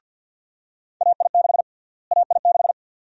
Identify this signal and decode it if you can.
Morse code, heard as a keyed tone, about 51 wpm, 700 Hz. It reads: AI6 AI6
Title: USA Call Sign Prefixes - Repeated 2x - 50wpm